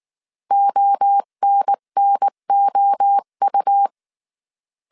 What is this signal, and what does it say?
Morse code, heard as a keyed tone, about 19 wpm, 800 Hz. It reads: ONNOU
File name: CW / DET / 1 3 1500C S B 18wpm 1916